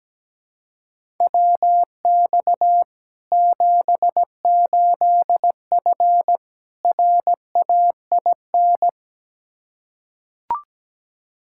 Morse code, heard as a keyed tone, about 17 wpm, 700 Hz. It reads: WX 78F RAIN
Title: K1USN SST QSO Practice 17wpm